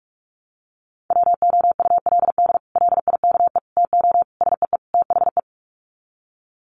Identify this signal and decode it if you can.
Morse code, heard as a keyed tone, about 44 wpm, 690 Hz. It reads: WOULD LIKE TO SEE THE
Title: Sets of 5 Words Encore2 45wpm